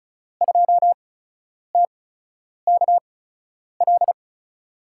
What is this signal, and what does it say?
Morse code, heard as a keyed tone, about 36 wpm, spaced as for 10 wpm, 700 Hz. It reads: 2TKL
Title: USA Call Signs - 5 Characters - 8x Character Spacing - 35wpm